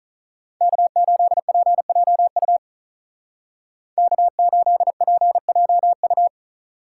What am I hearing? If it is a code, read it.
Morse code, heard as a keyed tone, about 46 wpm, 700 Hz. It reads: K8PJU K8PJU